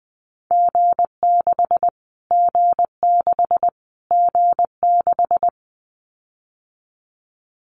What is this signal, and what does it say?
Morse code, heard as a keyed tone, about 20 wpm, 700 Hz. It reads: G6 G6 G6
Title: Callsign Prefixes for Top 25 DXCC - Repeated 3x - 20wpm